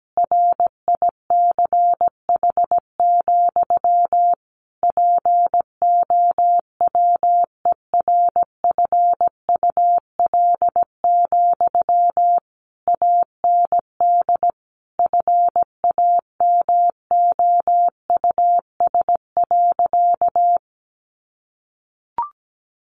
Morse code, heard as a keyed tone, about 17 wpm, 700 Hz. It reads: RICH, POWERFUL, AND FAMOUS.
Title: Sentences from Top 1500 Words 17wpm